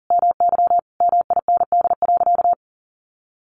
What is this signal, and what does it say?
Morse code, heard as a keyed tone, about 40 wpm, 700 Hz. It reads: MY MIND.
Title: Sentences from Top 1300 Words 40wpm